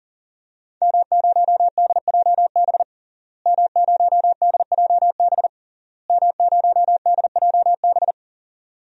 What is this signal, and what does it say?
Morse code, heard as a keyed tone, about 39 wpm, 700 Hz. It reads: M0DJB M0DJB M0DJB
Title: DXCC-100 Call Signs - 5 Characters - Repeated 3x - 40wpm